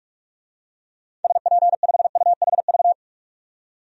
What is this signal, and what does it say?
Morse code, heard as a keyed tone, about 42 wpm, 700 Hz. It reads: SP5UHV
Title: DXCC-100 Call Signs - 6 Characters - 45wpm